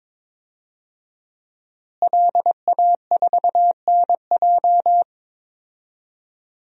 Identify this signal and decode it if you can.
Morse code, heard as a keyed tone, about 21 wpm, 700 Hz. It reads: LA4NJ
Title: DXCC-100 Call Signs - 5 Characters - Repeated 2x - 22wpm